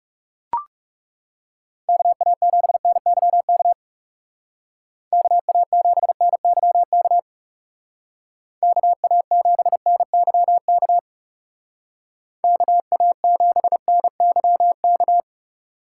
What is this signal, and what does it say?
Morse code, heard as a keyed tone, about 37 wpm, 700 Hz. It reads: KA7NYK KA7NYK KA7NYK KA7NYK
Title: Speed Racing - USA 2x3 Call Signs 30wpm